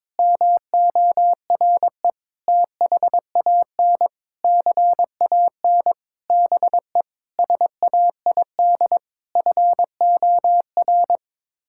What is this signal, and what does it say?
Morse code, heard as a keyed tone, about 22 wpm, 700 Hz. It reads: MORE THAN CAN BE SAID FOR
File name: practice / 22wpm / Sets of 8 Words - 22wpm